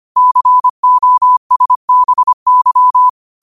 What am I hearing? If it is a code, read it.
Morse code, heard as a keyed tone, about 25 wpm, 1000 Hz. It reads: COSBY
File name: qotdfiles / 25wpm